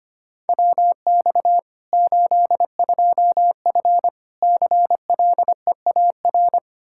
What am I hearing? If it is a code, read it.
Morse code, heard as a keyed tone, about 25 wpm, 700 Hz. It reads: WX 82F CLEAR
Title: Practice QSO - Exchange 2 - Info - 25wpm